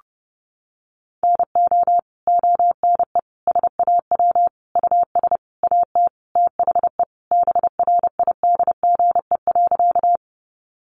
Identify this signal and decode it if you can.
Morse code, heard as a keyed tone, about 30 wpm, 700 Hz. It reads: NO ONE SAW US AT THE BRIDGE.
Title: Sentences from Top 1200 Words 30wpm